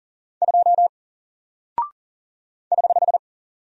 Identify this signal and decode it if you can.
Morse code, heard as a keyed tone, about 39 wpm, 700 Hz. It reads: 2 <HH>
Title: Single Character (Letter:Number:Punctuation:Prosign) 40wpm